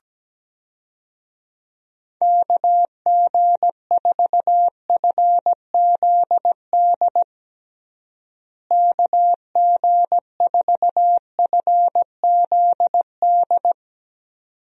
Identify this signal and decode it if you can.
Morse code, heard as a keyed tone, about 17 wpm, 700 Hz. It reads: KG4FZD KG4FZD
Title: USA Call Signs - 6 Characters - Repeated 2x - 17wpm